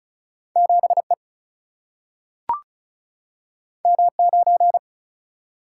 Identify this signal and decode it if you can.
Morse code, heard as a keyed tone, about 36 wpm, 700 Hz. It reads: 7E M9
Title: Single-Letter Single-Number 35wpm